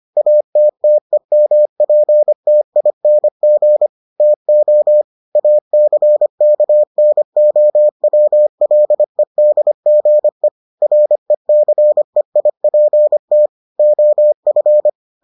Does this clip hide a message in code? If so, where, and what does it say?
Yes: Morse code from 0.2 to 14.9 s, heard as a keyed tone, about 25 wpm, 600 Hz. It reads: ATTEMPTING TO ACKNOWLEDGE RECEIPT OF